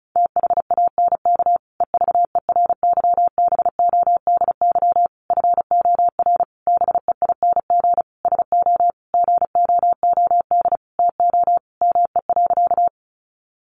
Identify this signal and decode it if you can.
Morse code, heard as a keyed tone, about 34 wpm, 700 Hz. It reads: THANK EVERYBODY FOR BEING SO GOOD TO ME.
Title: Sentences from Top 1400 Words 35wpm